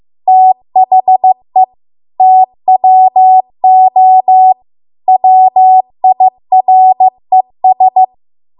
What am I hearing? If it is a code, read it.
Morse code, heard as a keyed tone, about 15 wpm, 750 Hz. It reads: THE TWO WIRES
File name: Archive / 15 WPM / 15WPM